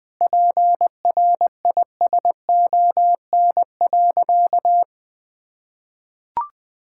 Morse code, heard as a keyed tone, about 21 wpm, 700 Hz. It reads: PRISON.
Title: Sentences with 8-Word Phrases - 20wpm